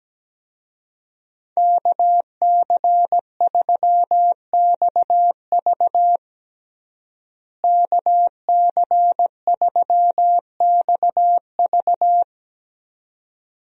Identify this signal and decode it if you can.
Morse code, heard as a keyed tone, about 17 wpm, 700 Hz. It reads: KC3XV KC3XV
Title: USA Call Signs - 5 Characters - Repeated 2x - 17wpm